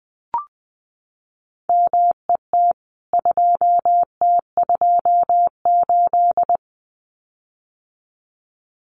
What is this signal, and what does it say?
Morse code, heard as a keyed tone, about 20 wpm, 700 Hz. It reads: MET 2T28
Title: CWT Exchange 20wpm